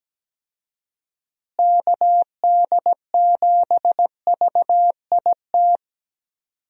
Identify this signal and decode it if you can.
Morse code, heard as a keyed tone, about 17 wpm, 700 Hz. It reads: KD7VIT
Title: USA Call Signs - 6 Characters - Repeated 2x - 17wpm